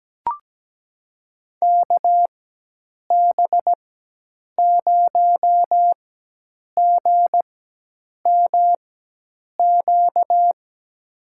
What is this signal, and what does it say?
Morse code, heard as a keyed tone, about 17 wpm, spaced as for 8 wpm, 700 Hz. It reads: KB0GMQ
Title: USA Call Signs - 6 Characters - 4x Character Spacing - 17wpm